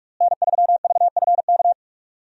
Morse code, heard as a keyed tone, about 46 wpm, 700 Hz. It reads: N3VFK
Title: USA Call Signs - 5 Characters - Repeated 2x - 45wpm